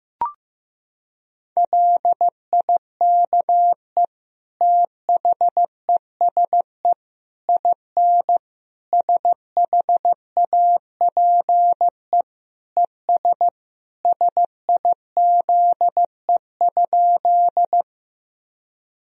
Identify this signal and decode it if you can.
Morse code, heard as a keyed tone, about 15 wpm, 700 Hz. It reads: LIKE THESE IN SHAPE ES SIZE?
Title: Sentences from Top 500 Words and Abbreviations 15wpm